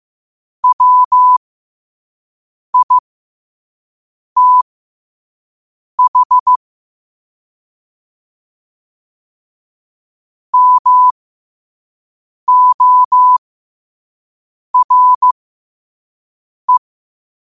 Morse code, heard as a keyed tone, about 15 wpm, spaced as for 5 wpm, 1000 Hz. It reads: WITH MORE
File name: qotdfiles / 5wpm